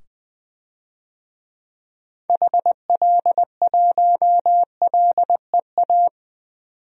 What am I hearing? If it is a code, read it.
Morse code, heard as a keyed tone, about 20 wpm, 700 Hz. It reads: HL1LEA
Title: DXCC-100 Call Signs - 6 Characters - 20wpm